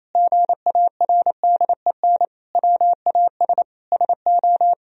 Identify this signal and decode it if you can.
Morse code, heard as a keyed tone, about 28 wpm, 700 Hz. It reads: GARDEN WAS SO
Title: Sentences from Top 1000 Words - Review Part 4 - 28wpm